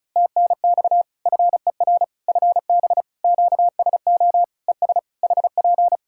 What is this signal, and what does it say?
Morse code, heard as a keyed tone, about 36 wpm, 700 Hz. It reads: TNX FER FB QSO ES HP
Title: Practice QSO - Exchange 4 - Ending - 35wpm